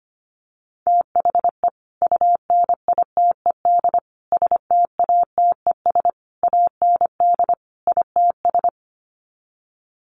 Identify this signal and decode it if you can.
Morse code, heard as a keyed tone, about 25 wpm, 700 Hz. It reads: THE UNITED STATES AND ITS
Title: Sets of 5 Words Encore 25wpm